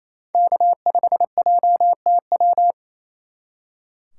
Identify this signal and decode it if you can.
Morse code, heard as a keyed tone, about 34 wpm, 700 Hz. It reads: K5JTW
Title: Speed Racing - USA 1x3 Call Signs 28wpm